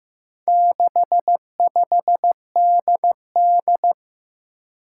Morse code, heard as a keyed tone, about 15 wpm, 700 Hz. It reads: 65DD
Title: Four Challenging Letter-Number Combinations 15wpm